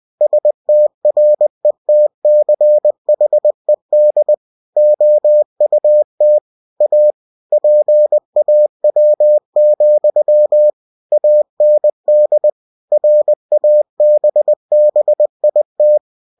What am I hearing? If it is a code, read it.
Morse code, heard as a keyed tone, about 20 wpm, 600 Hz. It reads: STRETCHED OUT A PAW, AND RABBIT
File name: cw-books / winnie-the-pooh / winnie-20wpm-page-0017